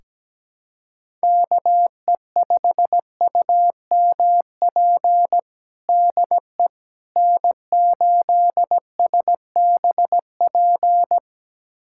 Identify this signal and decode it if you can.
Morse code, heard as a keyed tone, about 17 wpm, 700 Hz. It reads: KE5UMP DE N8SBP
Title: USA Callsign DE Callsign 17wpm